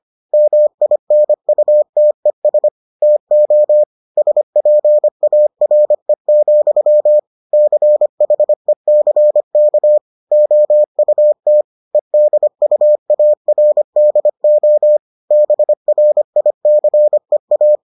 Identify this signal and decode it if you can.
Morse code, heard as a keyed tone, about 25 wpm, 600 Hz. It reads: MINUTES TO SPARE, CHECK OUT EDUARDO BRICEA